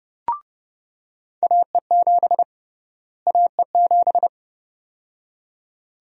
Morse code, heard as a keyed tone, about 30 wpm, 700 Hz. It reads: AE7 AE7
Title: USA Call Sign Prefixes - Repeated 2x - 30wpm